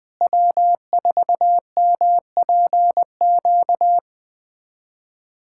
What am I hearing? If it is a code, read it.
Morse code, heard as a keyed tone, about 20 wpm, 700 Hz. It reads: W4MPQ
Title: USA Callsign DE Callsign - 5 Characters - 20wpm